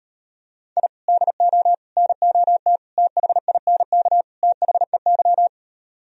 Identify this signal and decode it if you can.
Morse code, heard as a keyed tone, about 38 wpm, 700 Hz. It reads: I DO NOT THINK THEY